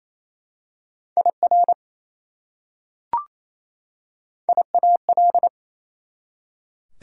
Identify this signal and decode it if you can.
Morse code, heard as a keyed tone, about 27 wpm, 700 Hz. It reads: IR IAL
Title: Prefixes and Suffixes 28wpm